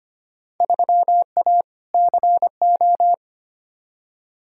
Morse code, heard as a keyed tone, about 25 wpm, 700 Hz. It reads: 3A CO
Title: Field Day Exchanges Encore - 25wpm